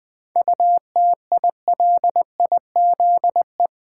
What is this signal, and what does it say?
Morse code, heard as a keyed tone, about 20 wpm, 690 Hz. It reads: UTILIZE
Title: 7 Letter Words - First 3 Letters Given - 20wpm